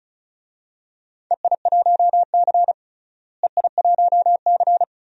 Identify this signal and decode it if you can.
Morse code, heard as a keyed tone, about 36 wpm, 700 Hz. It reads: EI1C EI1C
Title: DXCC-100 Call Signs - 4 Characters - Repeated 2x - 35wpm